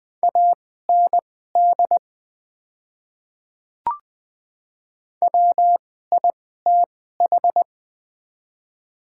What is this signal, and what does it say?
Morse code, heard as a keyed tone, about 21 wpm, spaced as for 15 wpm, 700 Hz. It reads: AND WITH
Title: 2x Character Spacing - Top 100 Words - 20wpm